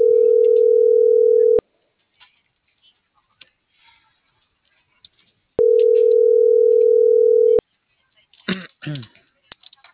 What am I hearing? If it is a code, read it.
Ringback tone (North America), 440 + 480 Hz.